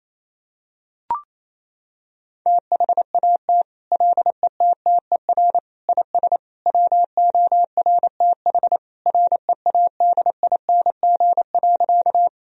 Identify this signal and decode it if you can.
Morse code, heard as a keyed tone, about 27 wpm, 700 Hz. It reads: THAT LETTER IS WORTH READING.
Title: Sentences from Top 1000 Words - Review Part 3 - 28wpm